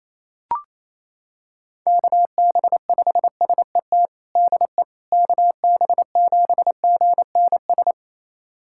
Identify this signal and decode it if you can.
Morse code, heard as a keyed tone, about 28 wpm, 700 Hz. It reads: KB5SET DE KB7GNS
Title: USA Callsign DE Callsign - 6 Characters - 28wpm